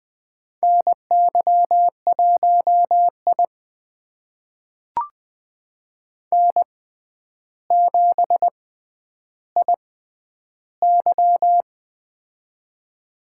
Morse code, heard as a keyed tone, about 20 wpm, spaced as for 10 wpm, 700 Hz. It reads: NY1I N7IY
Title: USA Call Signs - 4 Characters - 6x Character Spacing - 20wpm